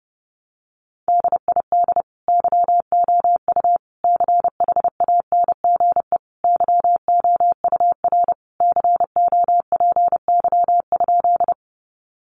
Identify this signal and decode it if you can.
Morse code, heard as a keyed tone, about 28 wpm, 700 Hz. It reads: DID YOU CHANGE YOUR COPY?
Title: Sentences from Top 1400 Words 30wpm